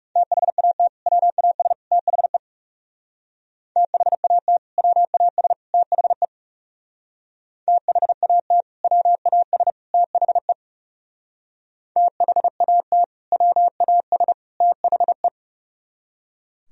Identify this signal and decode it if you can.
Morse code, heard as a keyed tone, about 37 wpm, 700 Hz. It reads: THAT WAS THE THAT WAS THE THAT WAS THE THAT WAS THE
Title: Speed-Racing - Sets of 3 Words 30wpm